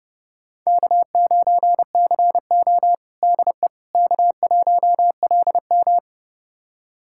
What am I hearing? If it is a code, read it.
Morse code, heard as a keyed tone, about 30 wpm, 700 Hz. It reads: K9CO DE K1LM
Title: USA Callsign DE Callsign - 4 Characters - 30wpm